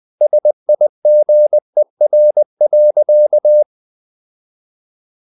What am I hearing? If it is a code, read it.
Morse code, heard as a keyed tone, about 20 wpm, 600 Hz. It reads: SIGER.
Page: Ferie 20 wpm